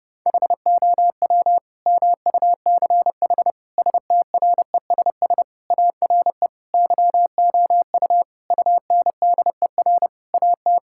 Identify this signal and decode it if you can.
Morse code, heard as a keyed tone, about 30 wpm, 700 Hz. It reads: HOW MUCH STRESS ARE YOU UNDER AT
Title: Sentences from Top 1000 Words - Review Part 6 - 30wpm